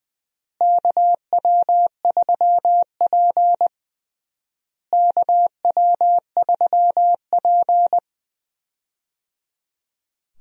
Morse code, heard as a keyed tone, about 20 wpm, 700 Hz. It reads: KW3P KW3P
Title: USA Call Signs - 4 Characters - Repeated 3x - 20wpm